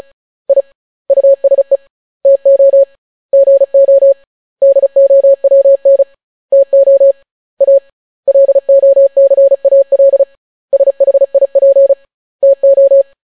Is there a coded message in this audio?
Yes: Morse code from 0.5 to 13.0 s, heard as a keyed tone, about 36 wpm, 550 Hz. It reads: I USE TO GO DOWN TO A LOCAL SHIP TO